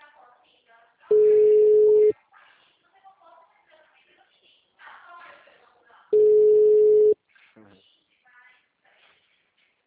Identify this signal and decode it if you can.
Ringback tone (Europe, 425 Hz), with speech.